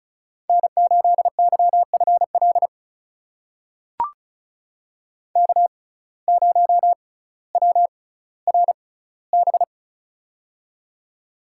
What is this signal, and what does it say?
Morse code, heard as a keyed tone, about 35 wpm, spaced as for 18 wpm, 700 Hz. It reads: N8YFL K0WRB
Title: USA Call Signs - 5 Characters - 6x Character Spacing - 35wpm